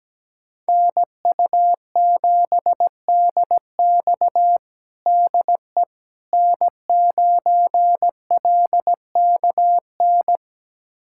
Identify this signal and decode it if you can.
Morse code, heard as a keyed tone, about 17 wpm, 700 Hz. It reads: NU7DX DE N9LKN
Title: USA Callsign DE Callsign - 5 Characters - 17wpm